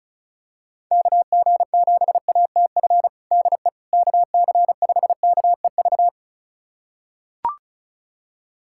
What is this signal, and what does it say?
Morse code, heard as a keyed tone, about 34 wpm, 700 Hz. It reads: KG7ATF DE KC5KEV
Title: USA Callsign DE Callsign - 6 Characters - 35wpm